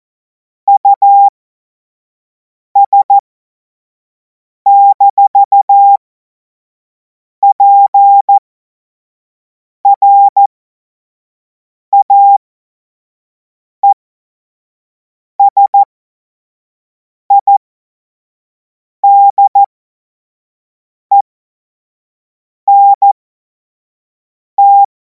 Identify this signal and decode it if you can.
Morse code, heard as a keyed tone, about 14 wpm, spaced as for 5 wpm, 800 Hz. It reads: US-PRAESIDENT
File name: Morsekurs / Witze / Witz 30. Juni 2026 25 F70